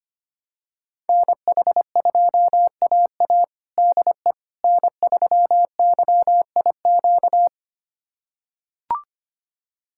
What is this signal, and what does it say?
Morse code, heard as a keyed tone, about 25 wpm, 700 Hz. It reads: NH2AA DE N3YIQ
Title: USA Callsign DE Callsign - 5 Characters - 25wpm